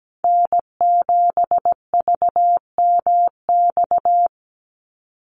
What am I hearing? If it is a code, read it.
Morse code, heard as a keyed tone, about 17 wpm, 700 Hz. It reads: N7VMX
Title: USA Call Signs 17wpm